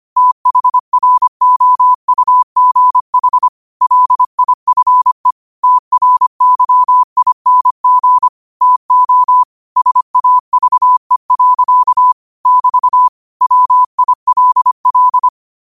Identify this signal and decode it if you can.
Morse code, heard as a keyed tone, about 25 wpm, 1000 Hz. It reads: THROUGH LIFE TRYING TO SAVE. <BT> WILL